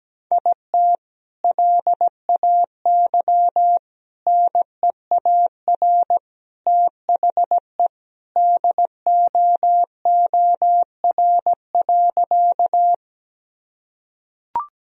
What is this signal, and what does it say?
Morse code, heard as a keyed tone, about 17 wpm, 700 Hz. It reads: IT LAY NEAR THE DOOR.
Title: Sentences from Top 1000 Words - Review Part 2 - 17wpm